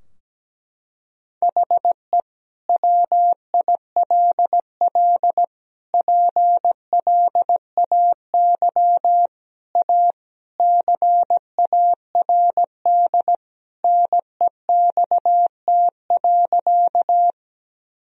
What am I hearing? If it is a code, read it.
Morse code, heard as a keyed tone, about 17 wpm, 700 Hz. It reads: HE WILL PLAY A CARD NEXT.
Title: Sentences from Top 1000 Words - Review Part 4 - 17wpm